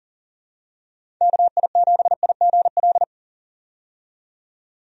Morse code, heard as a keyed tone, about 40 wpm, 700 Hz. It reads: KI7IGL
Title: USA Call Signs - 6 Characters - 40wpm